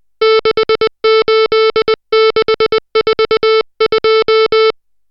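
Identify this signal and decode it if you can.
Morse code, heard as a keyed tone, about 20 wpm, 440 Hz, with noise. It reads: 68642
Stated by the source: Morse code drill of digits at 60 characters per minute.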